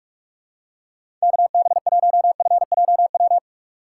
Morse code, heard as a keyed tone, about 42 wpm, 700 Hz. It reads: KB1FJW
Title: USA Call Signs - 6 Characters - 4x Character Spacing - 45wpm